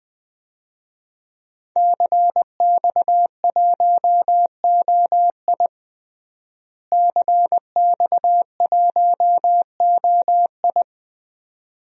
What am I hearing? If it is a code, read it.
Morse code, heard as a keyed tone, about 20 wpm, 700 Hz. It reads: CX1OI CX1OI
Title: DXCC-100 Call Signs - 5 Characters - Repeated 2x - 20wpm